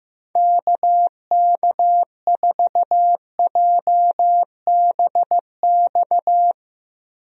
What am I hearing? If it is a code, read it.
Morse code, heard as a keyed tone, about 16 wpm, 700 Hz. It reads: KK4JBX
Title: USA Call Signs - 6 Characters - Repeated 2x - 15wpm